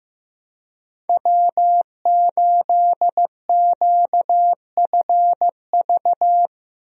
Morse code, heard as a keyed tone, about 15 wpm, 700 Hz. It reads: W8QFV
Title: USA Call Signs - 5 Characters - Repeated 3x - 15wpm